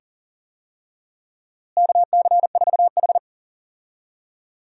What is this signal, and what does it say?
Morse code, heard as a keyed tone, about 40 wpm, 700 Hz. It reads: KC4H
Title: USA Call Signs - 4 Characters - Repeated 2x - 40wpm